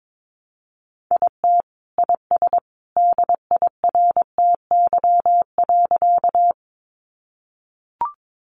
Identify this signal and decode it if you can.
Morse code, heard as a keyed tone, about 22 wpm, 700 Hz. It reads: IT IS DIRTY.
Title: Additional Easy Phrases 22wpm